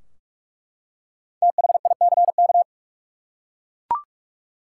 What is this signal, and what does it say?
Morse code, heard as a keyed tone, about 45 wpm, 700 Hz. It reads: THICK
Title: Five Letter Words - 45wpm